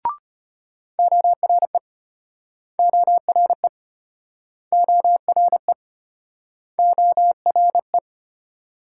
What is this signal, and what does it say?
Morse code, heard as a keyed tone, about 32 wpm, 700 Hz. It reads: ORE ORE ORE ORE